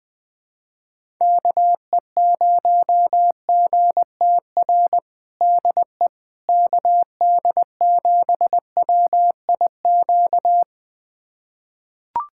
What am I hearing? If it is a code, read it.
Morse code, heard as a keyed tone, about 20 wpm, 700 Hz. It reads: KE0GTR DE KD7WIQ
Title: USA Callsign DE Callsign - 6 Characters - 20wpm